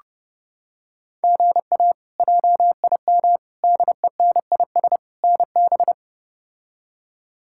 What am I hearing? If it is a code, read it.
Morse code, heard as a keyed tone, about 30 wpm, 700 Hz. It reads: GA JIM DENIS NB
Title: K1USN SST QSO Practice - 30wpm